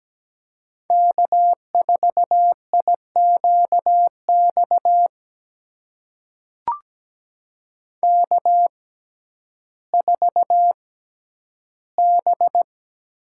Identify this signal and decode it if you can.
Morse code, heard as a keyed tone, about 17 wpm, spaced as for 10 wpm, 700 Hz. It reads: K4IQX K4B
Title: USA Call Signs - 5 Characters - 6x Character Spacing - 17wpm